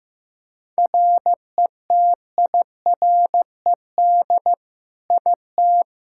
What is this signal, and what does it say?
Morse code, heard as a keyed tone, about 15 wpm, 700 Hz. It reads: RETIRED IT
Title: Practice QSO - Exchange 3 - More Info - 15wpm